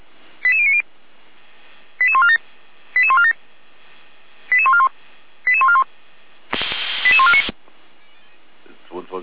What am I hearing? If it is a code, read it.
Five-tone selective call (ZVEI), heard as quick runs of five tones, tones of about 70 ms, with speech. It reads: 88090 88147 88147 88141 88141 88149